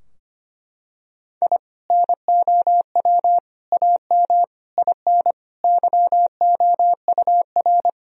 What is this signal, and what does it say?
Morse code, heard as a keyed tone, about 25 wpm, 700 Hz. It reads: I NOW AM IN YOUR
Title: Sentences from Top 1000 Words - Review Part 2 - 25wpm